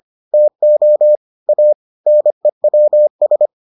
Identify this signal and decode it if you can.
Morse code, heard as a keyed tone, about 25 wpm, 600 Hz. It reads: TO A NEWS